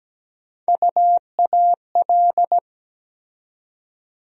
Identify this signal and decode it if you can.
Morse code, heard as a keyed tone, about 17 wpm, 700 Hz. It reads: UAL
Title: USA Call Sign Postfixes - 3 Letters 17wpm